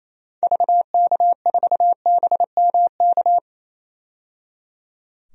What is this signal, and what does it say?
Morse code, heard as a keyed tone, about 28 wpm, 700 Hz. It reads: VK4BMK
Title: DXCC-100 Call Signs - 6 Characters - 28wpm